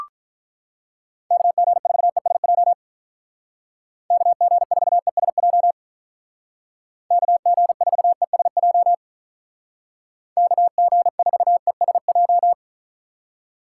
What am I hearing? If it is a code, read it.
Morse code, heard as a keyed tone, about 43 wpm, 700 Hz. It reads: KG4ESJ KG4ESJ KG4ESJ KG4ESJ
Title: Speed Racing - USA Call Signs 35wpm